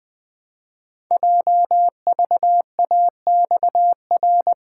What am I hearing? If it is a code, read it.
Morse code, heard as a keyed tone, about 20 wpm, 700 Hz. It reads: JVAXR
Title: Code Groups - 5 Letter:Numbers - 20wpm